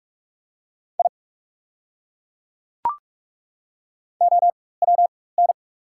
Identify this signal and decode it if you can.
Morse code, heard as a keyed tone, about 42 wpm, spaced as for 21 wpm, 690 Hz. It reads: I OWN
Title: 4x Character Spacing - Top 300 Words - 45wpm